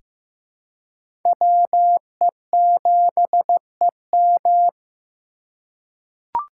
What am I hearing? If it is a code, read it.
Morse code, heard as a keyed tone, about 15 wpm, 700 Hz. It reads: WE7EM
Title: USA Call Signs - 5 Characters - 3x Character Spacing - 15wpm